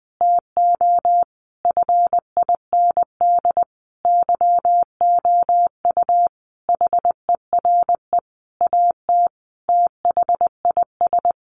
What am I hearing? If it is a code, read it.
Morse code, heard as a keyed tone, about 20 wpm, 700 Hz. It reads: TO FIND YOU HERE AT THIS
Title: Sentences from Top 1400 Words 20wpm